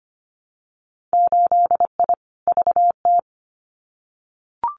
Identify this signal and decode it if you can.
Morse code, heard as a keyed tone, about 25 wpm, 700 Hz. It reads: 8I VT
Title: Winter Field Day Exchanges Encore 25wpm